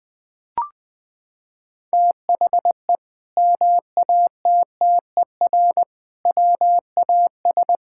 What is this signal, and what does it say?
Morse code, heard as a keyed tone, about 19 wpm, 700 Hz. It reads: THE MATTER WAS
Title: Sentences from Top 1000 Words - Review Part 5 - 20wpm